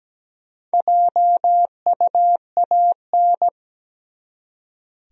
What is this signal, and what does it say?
Morse code, heard as a keyed tone, about 17 wpm, 700 Hz. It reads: JUAN
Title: USA Common Names - 17wpm